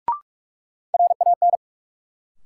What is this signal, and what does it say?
Morse code, heard as a keyed tone, about 46 wpm, 700 Hz. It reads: RAN
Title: Sentences from Top 500 Words - New Word Drill - 45wpm